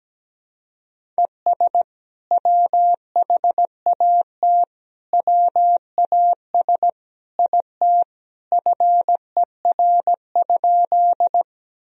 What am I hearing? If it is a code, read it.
Morse code, heard as a keyed tone, about 17 wpm, 700 Hz. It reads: ES WHAT WAS IT FER?
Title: Sentences from Top 100 Words and Abbreviations 17wpm